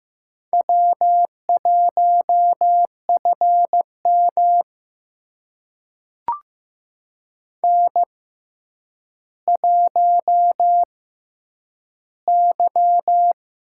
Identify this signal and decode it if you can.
Morse code, heard as a keyed tone, about 15 wpm, spaced as for 9 wpm, 700 Hz. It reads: W1FM N1Y
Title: USA Call Signs - 4 Characters - 6x Character Spacing - 15wpm